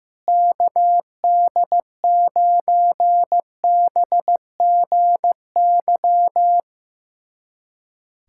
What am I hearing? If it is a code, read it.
Morse code, heard as a keyed tone, about 15 wpm, 700 Hz. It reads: KD9BGY
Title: USA Callsign DE Callsign - 6 Characters - 15wpm